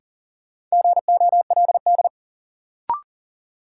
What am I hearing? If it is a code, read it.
Morse code, heard as a keyed tone, about 38 wpm, 700 Hz. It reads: GOLD
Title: Sentences from Top 1000 Words - Review Part 3 - New Word Drill - 40wpm